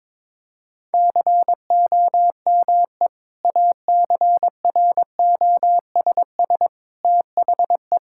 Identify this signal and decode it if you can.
Morse code, heard as a keyed tone, about 22 wpm, 700 Hz. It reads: COME ACROSS THE
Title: Sentences from Top 1000 Words - Review Part 5 - 22wpm